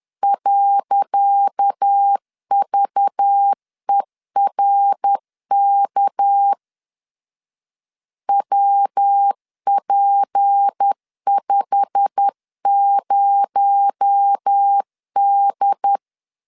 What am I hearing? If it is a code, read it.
Morse code, heard as a keyed tone, about 10 wpm, 800 Hz. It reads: .VERK WP50D
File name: LEZIONI / 10 SIGNS / 10C DOT 10wpm 2940